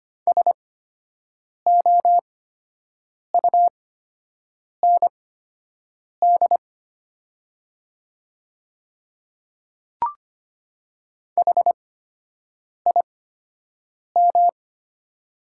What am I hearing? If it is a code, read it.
Morse code, heard as a keyed tone, about 25 wpm, spaced as for 7 wpm, 700 Hz. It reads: SOUND HIM
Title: 8x Character Spacing - Top 100 Words - 25wpm